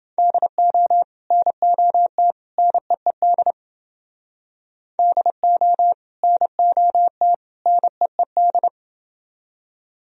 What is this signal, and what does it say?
Morse code, heard as a keyed tone, about 29 wpm, 700 Hz. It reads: DO NOT NEED DO NOT NEED